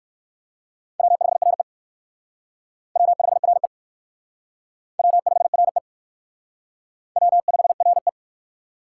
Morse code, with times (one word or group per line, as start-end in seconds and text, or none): 1.0-1.6 W5RE
3.0-3.7 W5RE
5.0-5.8 W5RE
7.2-8.1 W5RE